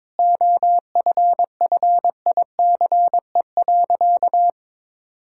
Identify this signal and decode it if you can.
Morse code, heard as a keyed tone, about 22 wpm, 700 Hz. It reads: OFFICE.
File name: practice / 22wpm / Sentences with 3-Word Phrases - 22wpm